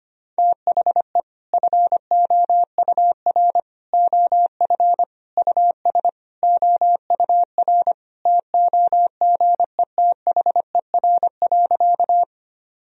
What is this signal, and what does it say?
Morse code, heard as a keyed tone, about 25 wpm, 700 Hz. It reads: THE FOUR OF US OUR TOGETHER.
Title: Sentences from Top 1000 Words - Review Part 1 - 25wpm